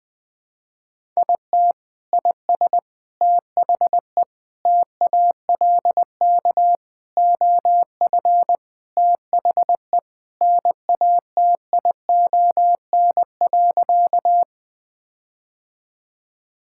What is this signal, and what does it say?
Morse code, heard as a keyed tone, about 20 wpm, 700 Hz. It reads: IT IS THE TALK OF THE NATION.
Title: Sentences with 4-Word Phrases - 20wpm